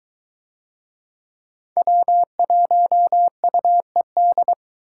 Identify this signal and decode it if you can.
Morse code, heard as a keyed tone, about 22 wpm, 700 Hz. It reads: W1UED